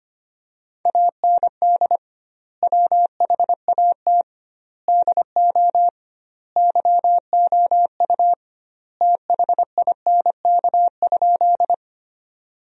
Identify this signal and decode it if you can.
Morse code, heard as a keyed tone, about 25 wpm, 700 Hz. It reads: AND WHAT DO YOU THINK?
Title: 2x Word-Spacing - Sentences from Top 200 Words - 25wpm